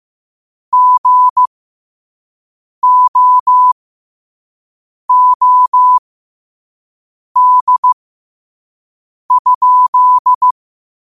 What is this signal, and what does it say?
Morse code, heard as a keyed tone, about 15 wpm, spaced as for 5 wpm, 1000 Hz. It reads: GOOD?
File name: 5wpm